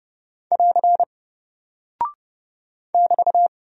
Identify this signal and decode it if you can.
Morse code, heard as a keyed tone, about 30 wpm, 700 Hz. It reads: <AR> <BT>
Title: Single Character (Letter:Number:Punctuation:Prosign) Alt 30wpm